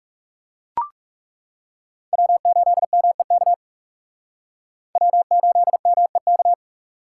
Morse code, heard as a keyed tone, about 35 wpm, 700 Hz. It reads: W8MEK W8MEK
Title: Speed Racing - USA 1x3 Call Signs 30wpm